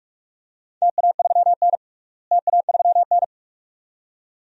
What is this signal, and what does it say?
Morse code, heard as a keyed tone, about 47 wpm, 700 Hz. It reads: TA3N TA3N
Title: DXCC-100 Call Signs - 4 Characters - Repeated 2x - 45wpm